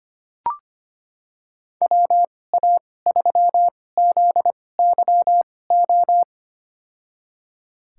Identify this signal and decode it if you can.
Morse code, heard as a keyed tone, about 25 wpm, spaced as for 18 wpm, 700 Hz. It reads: WA3ZYO
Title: USA Call Signs - 6 Characters - 2x Character Spacing - 25wpm